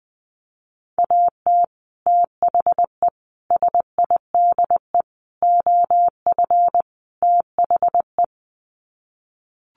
Morse code, heard as a keyed tone, about 20 wpm, 700 Hz. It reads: AT THE SIDE OF THE
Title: Sets of 5 Words Encore2 20wpm